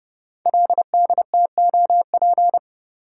Morse code, heard as a keyed tone, about 30 wpm, 700 Hz. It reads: LDTOP
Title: Code Groups - 5 Letters - 30wpm